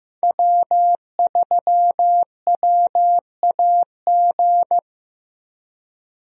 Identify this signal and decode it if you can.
Morse code, heard as a keyed tone, about 15 wpm, 700 Hz. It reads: W3WAG
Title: USA Call Signs - 5 Characters - Repeated 2x - 15wpm